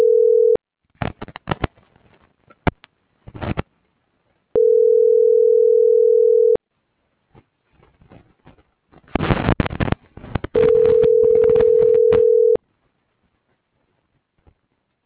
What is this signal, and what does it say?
Ringback tone (North America), 440 + 480 Hz.
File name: CENTRAL GENERAL / out tania.lemus - .55822